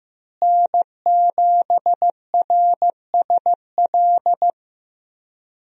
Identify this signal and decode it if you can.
Morse code, heard as a keyed tone, about 15 wpm, 700 Hz. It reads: N7RSL
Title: USA Call Signs - 5 Characters - Repeated 3x - 15wpm